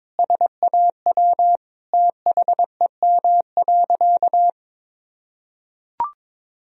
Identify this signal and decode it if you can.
Morse code, heard as a keyed tone, about 22 wpm, 700 Hz. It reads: SAW THEM.
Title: Sentences from Top 1000 Words - Review Part 2 - 22wpm